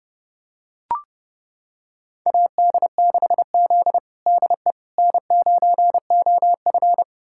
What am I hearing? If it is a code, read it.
Morse code, heard as a keyed tone, about 30 wpm, 700 Hz. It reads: AD6Z DE N9OF
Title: USA Callsign DE Callsign - 4 Characters - 30wpm